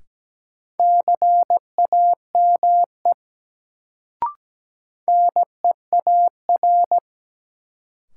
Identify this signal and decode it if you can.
Morse code, heard as a keyed tone, about 17 wpm, 700 Hz. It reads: CAME NEAR
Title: Sentences from Top 200 Words - New Word Drill - 17wpm